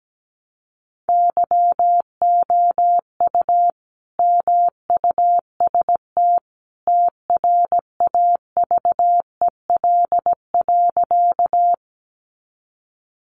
Morse code, heard as a keyed tone, about 17 wpm, 700 Hz. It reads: YOU MUST TRAVEL.
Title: Sentences from Top 400 Words 17wpm